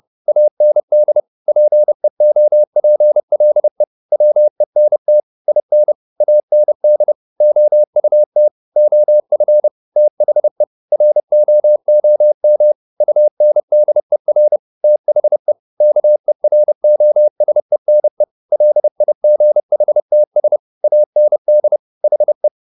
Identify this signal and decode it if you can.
Morse code, heard as a keyed tone, about 30 wpm, 600 Hz. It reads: AND PEOPLE WENT IN AND OUT OF THE ROOM UNDER THE KEROSENE LIGHTS AND HE
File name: the old man and the sea 2 30wpm